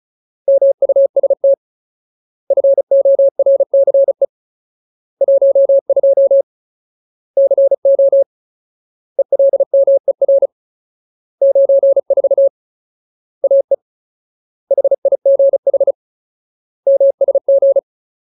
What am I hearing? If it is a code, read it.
Morse code, heard as a keyed tone, about 34 wpm, 550 Hz. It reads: MUST FORCE 12 CO ELMER 94 AE HIGH MSG
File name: Ultimate Word List / 35WPM Ultimate